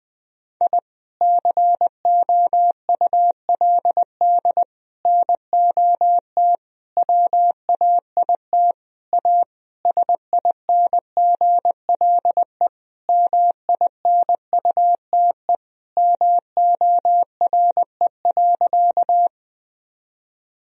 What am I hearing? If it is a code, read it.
Morse code, heard as a keyed tone, about 21 wpm, 700 Hz. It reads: I COULD NOT WAIT A SINGLE MINUTE MORE.
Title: Sentences from Top 1000 Words - Review Part 4 - 20wpm